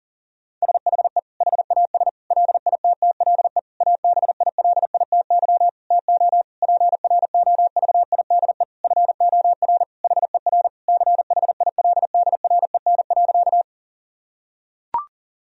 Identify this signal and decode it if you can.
Morse code, heard as a keyed tone, about 40 wpm, 700 Hz. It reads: SHE HAS LITTLE ABILITY TO PROVIDE FOR HER CHILDREN.
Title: Sentences from Top 1000 Words - Review Part 3 - 40wpm